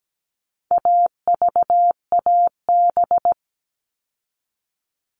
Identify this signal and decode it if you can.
Morse code, heard as a keyed tone, about 17 wpm, 700 Hz. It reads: AVAB
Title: All Cut Numbers - 4-Digit 17wpm